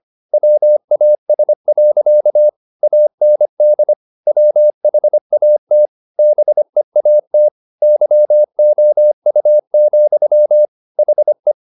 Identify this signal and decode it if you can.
Morse code, heard as a keyed tone, about 25 wpm, 600 Hz. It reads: WAS. AND WHAT BEAT YOU, HE